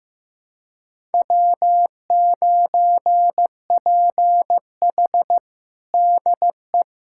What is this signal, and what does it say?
Morse code, heard as a keyed tone, about 15 wpm, 700 Hz. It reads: W9PH DE
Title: USA Callsign DE Callsign - 4 Characters - 15wpm